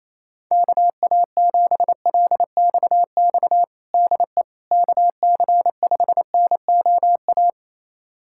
Morse code, heard as a keyed tone, about 28 wpm, 700 Hz. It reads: KA7LXX DE KC5NOA
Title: USA Callsign DE Callsign 28wpm